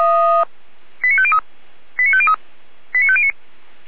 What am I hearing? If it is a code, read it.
Five-tone selective call (ZVEI), heard as quick runs of five tones, tones of about 70 ms. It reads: 89503 89503 89509